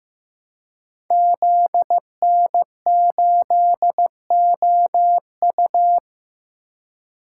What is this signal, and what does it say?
Morse code, heard as a keyed tone, about 15 wpm, 700 Hz. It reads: ZN8OU
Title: Code Groups - 5 Letter:Numbers - 15wpm